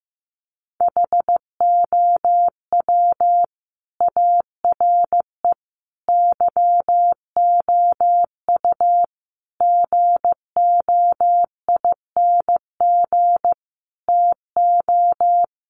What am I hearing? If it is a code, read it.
Morse code, heard as a keyed tone, about 15 wpm, 700 Hz. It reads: HOW ARE YOU GOING TO
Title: Sets of 5 Words 15wpm